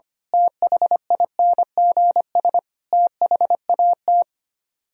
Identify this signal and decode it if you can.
Morse code, heard as a keyed tone, about 24 wpm, 700 Hz. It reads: THINGS THAT
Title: Speed-Racing - Sets of 5 Words 25wpm